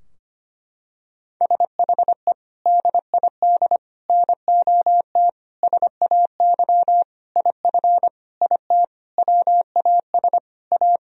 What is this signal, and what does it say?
Morse code, heard as a keyed tone, about 25 wpm, 700 Hz. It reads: SHE DID NOT SAY IF IT WAS A
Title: Sentences from Top 1000 Words - Review Part 3 - 25wpm